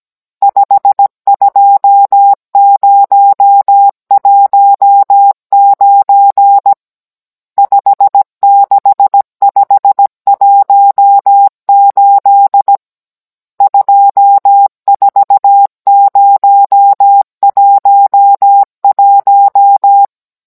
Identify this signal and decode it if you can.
Morse code, heard as a keyed tone, about 17 wpm, 800 Hz. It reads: 52019 56518 24011